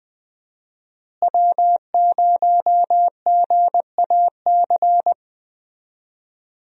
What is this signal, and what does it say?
Morse code, heard as a keyed tone, about 20 wpm, 700 Hz. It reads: W0GAC
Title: USA Call Signs - 5 Characters - Repeated 2x - 20wpm